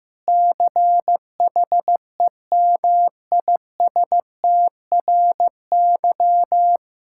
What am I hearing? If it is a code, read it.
Morse code, heard as a keyed tone, about 16 wpm, 700 Hz. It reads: CHEMISTRY
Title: Sentences from Top 1000 Words - Review Part 6 - 15wpm